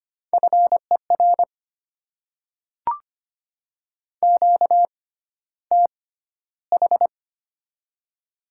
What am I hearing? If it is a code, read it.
Morse code, heard as a keyed tone, about 25 wpm, spaced as for 13 wpm, 700 Hz. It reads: FER QTH
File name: Top 20 QSO Elements - 6x Character Spacing - 25wpm